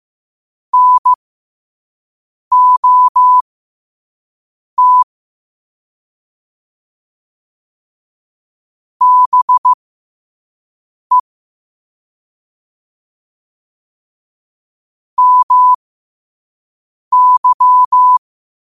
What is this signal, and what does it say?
Morse code, heard as a keyed tone, about 15 wpm, spaced as for 5 wpm, 1000 Hz. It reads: NOT BE MY
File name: qotdfiles / 5wpm